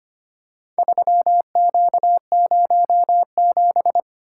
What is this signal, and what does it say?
Morse code, heard as a keyed tone, about 25 wpm, 700 Hz. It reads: 3Q07
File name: practice / 25wpm / Four Letter-Number Combinations - 25wpm